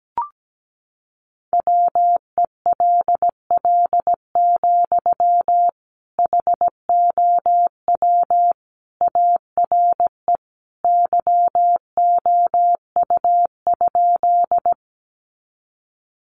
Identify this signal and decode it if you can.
Morse code, heard as a keyed tone, about 17 wpm, 700 Hz. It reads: WELL, HOW ARE YOU?
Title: Sentences from Top 200 Words 17wpm